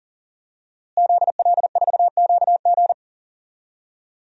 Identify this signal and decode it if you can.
Morse code, heard as a keyed tone, about 40 wpm, 700 Hz. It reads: ZL4QG
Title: DXCC-100 Call Signs - 5 Characters - Repeated 2x - 40wpm